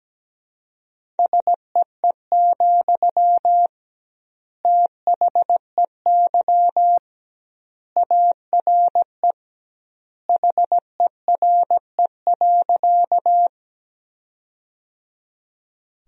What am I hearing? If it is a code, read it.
Morse code, heard as a keyed tone, about 17 wpm, 700 Hz. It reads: SEE, THEY ARE HERE.
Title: 2x Word-Spacing - Sentences from Top 200 Words - 17wpm